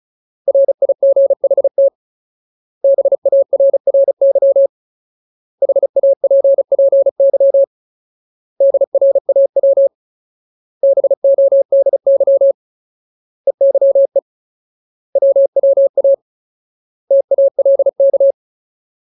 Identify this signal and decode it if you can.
Morse code, heard as a keyed tone, about 35 wpm, 550 Hz. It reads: RIGHT BARRY HAPPY DRAW BODY EYE WWA TALK
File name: Ultimate Word List / 35WPM Ultimate